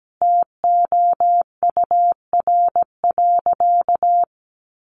Morse code, heard as a keyed tone, about 17 wpm, 700 Hz. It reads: TOUR.
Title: Sentences from Top 1400 Words 17wpm